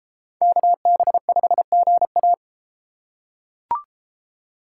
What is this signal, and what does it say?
Morse code, heard as a keyed tone, about 30 wpm, 700 Hz. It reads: KB5GA